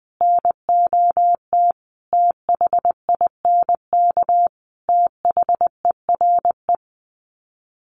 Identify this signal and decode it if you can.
Morse code, heard as a keyed tone, about 20 wpm, 700 Hz. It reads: NOT THINK THERE
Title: Sets of 4 Words 20wpm